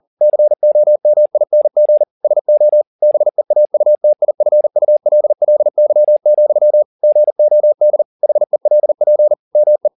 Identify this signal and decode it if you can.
Morse code, heard as a keyed tone, about 40 wpm, 600 Hz. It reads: COMING SO BEAUTIFULLY, GOD HELP ME
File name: the old man and the sea 2 40wpm